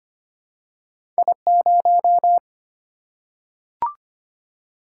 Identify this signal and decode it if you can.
Morse code, heard as a keyed tone, about 25 wpm, 700 Hz. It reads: I0
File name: Two Letter-Number Combinations - 25wpm